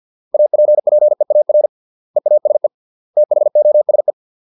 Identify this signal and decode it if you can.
Morse code, heard as a keyed tone, about 50 wpm, 600 Hz. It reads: APPEAR EASE THOSE